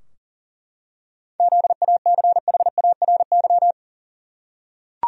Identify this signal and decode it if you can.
Morse code, heard as a keyed tone, about 40 wpm, 700 Hz. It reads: ZACHARY
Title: USA Common Names - 40wpm